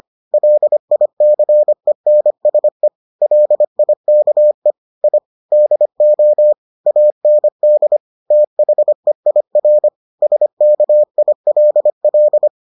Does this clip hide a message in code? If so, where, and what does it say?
Yes: Morse code from 0.3 to 12.5 s, heard as a keyed tone, about 25 wpm, 600 Hz. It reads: LICENSE LIKE I DO AND THEIR SKILL